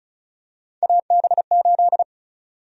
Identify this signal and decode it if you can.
Morse code, heard as a keyed tone, about 36 wpm, 700 Hz. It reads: AB8
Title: USA Call Sign Prefixes - Repeated 2x - 35wpm